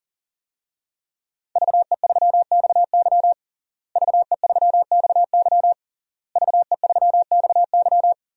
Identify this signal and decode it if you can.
Morse code, heard as a keyed tone, about 40 wpm, 700 Hz. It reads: VE3XY VE3XY VE3XY
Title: DXCC-100 Call Signs - 5 Characters - Repeated 3x - 40wpm